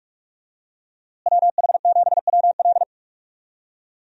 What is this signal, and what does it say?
Morse code, heard as a keyed tone, about 46 wpm, 700 Hz. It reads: WH7WL
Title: USA Call Signs - 5 Characters - Repeated 3x - 45wpm